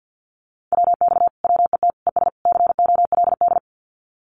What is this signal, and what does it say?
Morse code, heard as a keyed tone, about 50 wpm, 700 Hz. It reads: WX WET ES COLD
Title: Common QSO Phrases Encore 50wpm